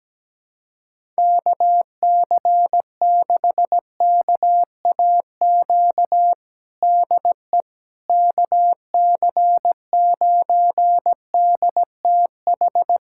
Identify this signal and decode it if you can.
Morse code, heard as a keyed tone, about 17 wpm, 700 Hz. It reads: KC6KAQ DE KC9DTH
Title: USA Callsign DE Callsign - 6 Characters - 17wpm